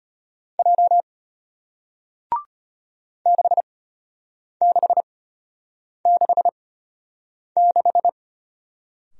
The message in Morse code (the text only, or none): J 6666